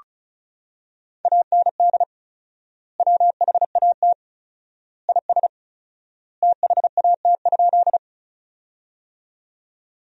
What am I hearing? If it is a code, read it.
Morse code, heard as a keyed tone, about 36 wpm, 700 Hz. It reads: AND WHAT IS THAT?
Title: 4x Word-Spacing - Sentences from Top 100 Words - 35wpm